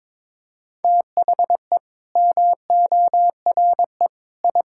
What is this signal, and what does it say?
Morse code, heard as a keyed tone, about 21 wpm, 700 Hz. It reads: THE MORE I
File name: Sets of 9 Words - 22wpm